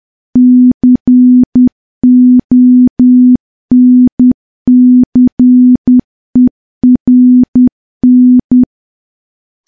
Morse code, heard as a keyed tone, about 10 wpm, 250 Hz. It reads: CONCERN
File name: morse / morse-training-10WPM